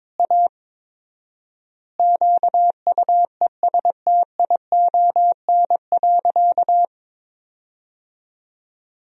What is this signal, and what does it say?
Morse code, heard as a keyed tone, about 22 wpm, 700 Hz. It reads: A QUESTION.
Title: 4x Word-Spacing - Sentences from Top 400 Words - 22wpm